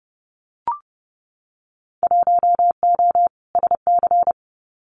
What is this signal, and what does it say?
Morse code, heard as a keyed tone, about 30 wpm, 700 Hz. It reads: 1O SC
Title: Winter Field Day Exchanges 30wpm